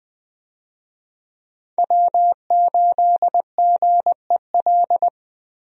Morse code, heard as a keyed tone, about 20 wpm, 700 Hz. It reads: W8GEL
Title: USA Call Signs - 5 Characters - Repeated 2x - 20wpm